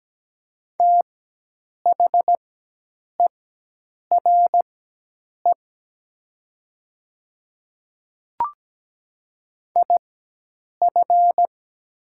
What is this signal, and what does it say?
Morse code, heard as a keyed tone, about 17 wpm, spaced as for 8 wpm, 700 Hz. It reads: THERE IF
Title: 4x Character Spacing - Top 100 Words - 17wpm